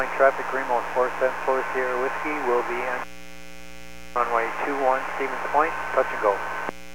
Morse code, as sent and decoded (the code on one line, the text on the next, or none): none